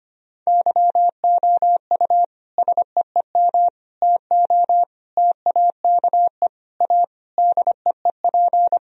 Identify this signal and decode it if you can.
Morse code, heard as a keyed tone, about 25 wpm, 700 Hz. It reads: YOU SEEM TO TAKE A DEEP
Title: Sentences from Top 1000 Words - Review Part 3 - 25wpm